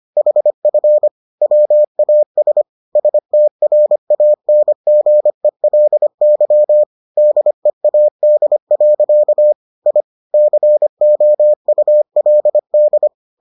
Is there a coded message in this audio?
Yes: Morse code from 0.2 to 13.1 s, heard as a keyed tone, about 25 wpm, 600 Hz. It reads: HF WAS STRANGELY DEAD. I COULD